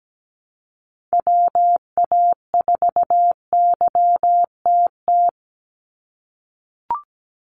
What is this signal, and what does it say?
Morse code, heard as a keyed tone, about 17 wpm, 700 Hz. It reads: WA4YTT
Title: USA Call Signs 17wpm